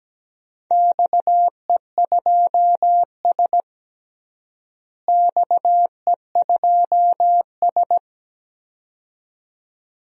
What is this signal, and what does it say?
Morse code, heard as a keyed tone, about 17 wpm, 700 Hz. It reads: XE2S XE2S
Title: DXCC-100 Call Signs - 4 Characters - Repeated 2x - 17wpm